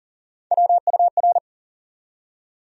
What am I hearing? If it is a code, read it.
Morse code, heard as a keyed tone, about 39 wpm, 700 Hz. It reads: WUR
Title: USA Call Sign Postfixes - 3 Letters 40wpm